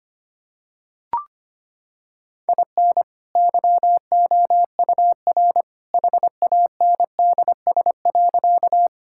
Morse code, heard as a keyed tone, about 25 wpm, 700 Hz. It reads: IN YOUR HANDS.
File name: practice / 25wpm / Sentences with 2-Word Phrases - 25wpm